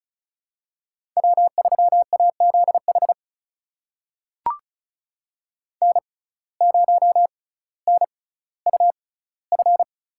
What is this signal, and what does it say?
Morse code, heard as a keyed tone, about 36 wpm, spaced as for 18 wpm, 700 Hz. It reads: W3AZH N0NUF
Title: USA Call Signs - 5 Characters - 6x Character Spacing - 35wpm